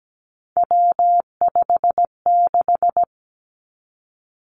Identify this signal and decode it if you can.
Morse code, heard as a keyed tone, about 17 wpm, 700 Hz. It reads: W56
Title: Three Challenging Letter-Number Combinations 17wpm